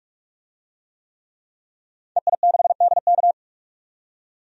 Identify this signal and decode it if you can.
Morse code, heard as a keyed tone, about 46 wpm, 700 Hz. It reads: EI6DK
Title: DXCC-100 Call Signs - 5 Characters - 45wpm